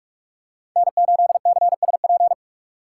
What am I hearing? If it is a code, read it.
Morse code, heard as a keyed tone, about 47 wpm, 700 Hz. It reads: N8CSP
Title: USA Call Signs - 3x Character Spacing - 45wpm